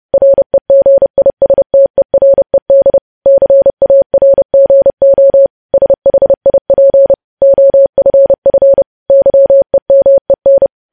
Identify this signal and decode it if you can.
Morse code, heard as a keyed tone, about 30 wpm, 550 Hz. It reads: REGISTERED CARGO SHIP OFF YEMEN
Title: News Headlines at 30